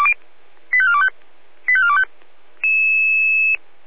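Five-tone selective call (ZVEI), heard as quick runs of five tones, tones of about 70 ms. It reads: 85425 85425
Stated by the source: ZVEI five-tone selective call received with an SDR.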